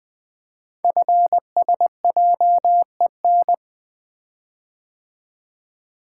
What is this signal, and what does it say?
Morse code, heard as a keyed tone, about 20 wpm, 700 Hz. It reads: FSJEN
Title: Code Groups - 5 Letters - 20wpm